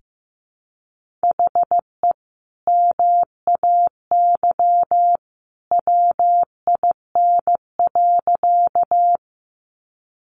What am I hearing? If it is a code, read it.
Morse code, heard as a keyed tone, about 15 wpm, 700 Hz. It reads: HE MAY WIN.
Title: Sentences from Top 1500 Words 15wpm